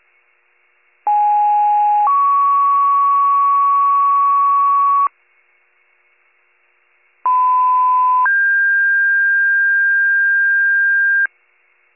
Two-tone page, heard as a one-second tone, then a three-second tone, about 830 then 1120 Hz, about 1010 then 1600 Hz. It is heard from 1.1 to 11.3 s.